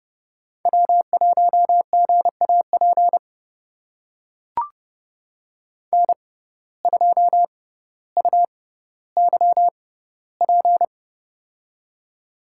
Morse code, heard as a keyed tone, about 31 wpm, spaced as for 16 wpm, 700 Hz. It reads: W1GAP N2UYP
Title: USA Call Signs - 5 Characters - 6x Character Spacing - 30wpm